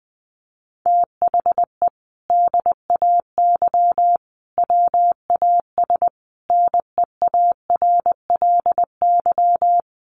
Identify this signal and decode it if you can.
Morse code, heard as a keyed tone, about 20 wpm, 700 Hz. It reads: THE DAY WAS NEARLY
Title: Sentences from Top 1300 Words 20wpm